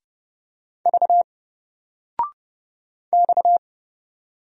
Morse code, heard as a keyed tone, about 30 wpm, 700 Hz. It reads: V X
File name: Single-Letter-Number-30wpm